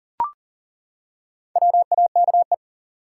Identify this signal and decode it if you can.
Morse code, heard as a keyed tone, about 40 wpm, 700 Hz. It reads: WAKE
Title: Four Letter Words - 40wpm